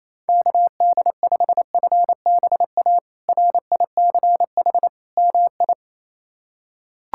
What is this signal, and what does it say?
Morse code, heard as a keyed tone, about 28 wpm, 700 Hz. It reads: KD5FBA RICH MI
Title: K1USN SST QSO Practice - 28wpm